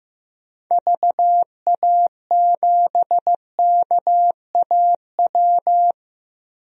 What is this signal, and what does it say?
Morse code, heard as a keyed tone, about 15 wpm, 700 Hz. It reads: VA7KAW
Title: DXCC-100 Call Signs - 6 Characters - Repeated 2x - 15wpm